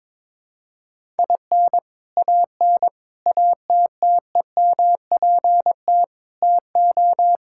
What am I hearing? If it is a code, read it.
Morse code, heard as a keyed tone, about 22 wpm, 700 Hz. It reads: IN AN ATTEMPT TO
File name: practice / 22wpm / Sentences with 4-Word Phrases - 22wpm